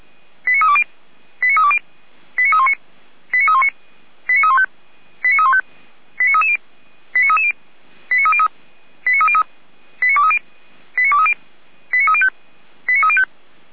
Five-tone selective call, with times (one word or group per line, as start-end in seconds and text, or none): 0.5-0.8 89320
1.4-1.8 89320
2.4-2.7 89319
3.3-3.7 89319
4.3-4.7 89315
5.2-5.6 89315
6.2-6.6 89330
7.1-7.5 89330
8.1-8.5 89303
9.1-9.4 89303
10.0-10.4 89230
11.0-11.3 89230
11.9-12.3 89305
12.9-13.2 89305